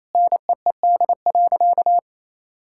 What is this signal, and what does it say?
Morse code, heard as a keyed tone, about 28 wpm, 700 Hz. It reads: NEED.
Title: Sentences from Top 1000 Words - Review Part 6 - 28wpm